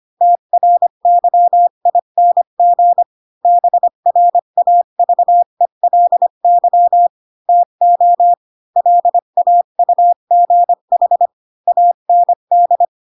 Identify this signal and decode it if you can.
Morse code, heard as a keyed tone, about 25 wpm, 700 Hz. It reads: TRYING BRAVELY TO LAUGH AND
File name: Pinocchio / Pinocchio ch1 20 25 ik1hge